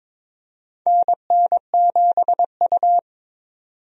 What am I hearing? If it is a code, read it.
Morse code, heard as a keyed tone, about 22 wpm, 700 Hz. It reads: NN7U
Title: USA Call Signs - 4 Characters - 22wpm